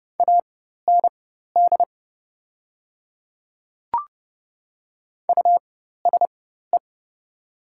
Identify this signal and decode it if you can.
Morse code, heard as a keyed tone, about 30 wpm, spaced as for 14 wpm, 700 Hz. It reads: AND USE
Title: 4x Character Spacing - Top 100 Words - 30wpm